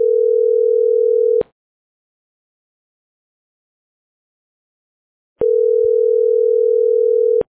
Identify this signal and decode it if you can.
Ringback tone (North America), 440 + 480 Hz.